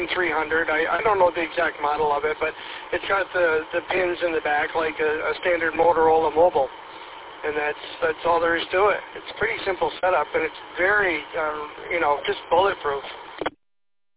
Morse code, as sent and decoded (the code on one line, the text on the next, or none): none